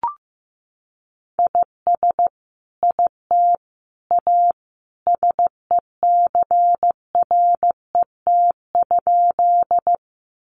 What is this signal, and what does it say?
Morse code, heard as a keyed tone, about 15 wpm, 700 Hz. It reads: IS IT A SECRET?
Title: Sentences from Top 1200 Words 15wpm